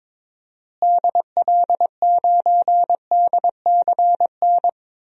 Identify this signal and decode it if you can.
Morse code, heard as a keyed tone, about 21 wpm, 700 Hz. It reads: DL9DCN
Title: DXCC-100 Call Signs - 6 Characters - Repeated 3x - 22wpm